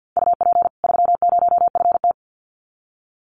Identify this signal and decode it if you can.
Morse code, heard as a keyed tone, about 50 wpm, 700 Hz. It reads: UP 30FT
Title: Common QSO Phrases 50wpm